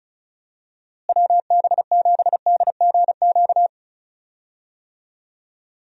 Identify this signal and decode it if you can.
Morse code, heard as a keyed tone, about 36 wpm, 700 Hz. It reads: WB7DGQ
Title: USA Call Signs - 6 Characters - Repeated 2x - 35wpm